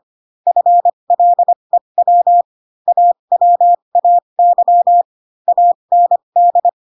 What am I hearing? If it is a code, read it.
Morse code, heard as a keyed tone, about 25 wpm, spaced as for 22 wpm, 700 Hz. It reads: FLEW AWAY AND
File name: Pinocchio / Pinocchio ch7 18 25 ik1hge